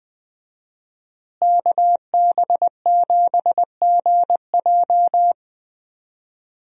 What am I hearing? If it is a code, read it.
Morse code, heard as a keyed tone, about 20 wpm, 700 Hz. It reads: KB7GJ
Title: USA Call Signs - 5 Characters - Repeated 3x - 20wpm